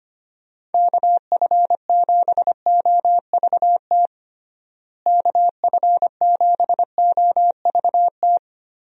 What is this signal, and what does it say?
Morse code, heard as a keyed tone, about 25 wpm, 700 Hz. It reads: KF7OVT KF7OVT
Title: USA Call Signs - 6 Characters - Repeated 2x - 25wpm